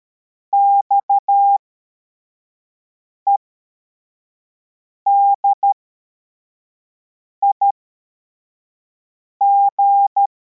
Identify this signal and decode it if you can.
Morse code, heard as a keyed tone, about 13 wpm, spaced as for 4 wpm, 800 Hz. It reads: XEDIG
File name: formation / cw / Text43